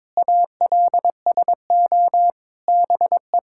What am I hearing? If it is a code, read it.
Morse code, heard as a keyed tone, about 22 wpm, 700 Hz. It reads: ALSO BE
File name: Sets of 9 Words - 22wpm